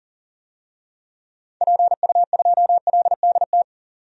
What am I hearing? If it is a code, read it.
Morse code, heard as a keyed tone, about 40 wpm, 700 Hz. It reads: PU2LDT
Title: DXCC-100 Call Signs - 6 Characters - 40wpm